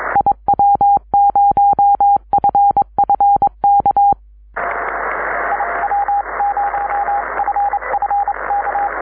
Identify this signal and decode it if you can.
Morse code, heard as a keyed tone, about 25 wpm, 810 Hz, with radio noise. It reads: IW0FFK IW0FFK
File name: iw0ffk / 2005 05 31 ey8mm iw0ffk cw